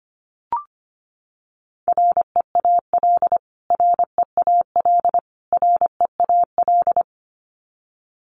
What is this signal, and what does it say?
Morse code, heard as a keyed tone, about 25 wpm, 700 Hz. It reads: REAL REAL REAL
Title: Top 300 Words Repeated 25wpm